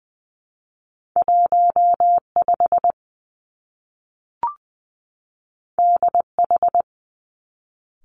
Morse code, heard as a keyed tone, about 20 wpm, 700 Hz. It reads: 1 5 D H
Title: Two Challenging Letter-Number Combinations 20wpm